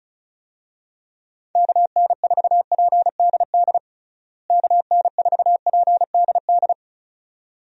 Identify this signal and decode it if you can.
Morse code, heard as a keyed tone, about 34 wpm, 700 Hz. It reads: KN4PDD KN4PDD
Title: USA Call Signs - 6 Characters - Repeated 2x - 35wpm